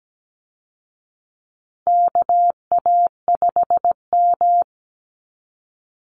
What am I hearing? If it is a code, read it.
Morse code, heard as a keyed tone, about 17 wpm, 700 Hz. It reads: KA5M
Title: Call Signs Repeated 3x - 17wpm